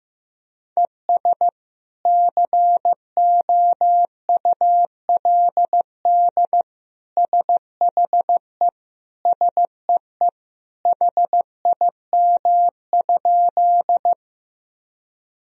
Morse code, heard as a keyed tone, about 15 wpm, 700 Hz. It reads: ES COULD SHE SEE HIM?
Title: Sentences from Top 100 Words and Abbreviations 15wpm